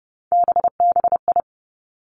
Morse code, heard as a keyed tone, about 30 wpm, 700 Hz. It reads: BBI
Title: Three Challenging Letter-Number Combinations 30wpm